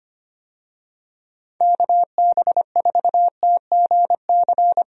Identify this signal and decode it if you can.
Morse code, heard as a keyed tone, about 25 wpm, 700 Hz. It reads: KB4TGC
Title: USA Call Signs - 6 Characters - Repeated 2x - 25wpm